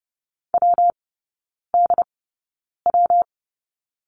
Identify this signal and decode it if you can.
Morse code, heard as a keyed tone, about 30 wpm, spaced as for 9 wpm, 700 Hz. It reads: WDW
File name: Lesson 19e - Letter w - Three Characters 30wpm